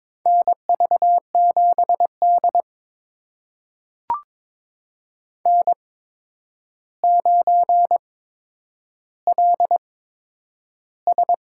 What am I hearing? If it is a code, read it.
Morse code, heard as a keyed tone, about 22 wpm, spaced as for 9 wpm, 700 Hz. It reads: NV7D N9LS
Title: USA Call Signs - 4 Characters - 8x Character Spacing - 22wpm